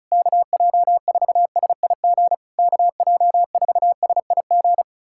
Morse code, heard as a keyed tone, about 34 wpm, 700 Hz. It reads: KJ4SIG KJ4SIG
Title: SOTA QSO Practice 35wpm